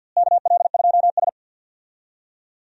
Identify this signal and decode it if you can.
Morse code, heard as a keyed tone, about 52 wpm, 700 Hz. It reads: KL2S
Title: USA Call Signs - 4 Characters - Repeated 2x - 50wpm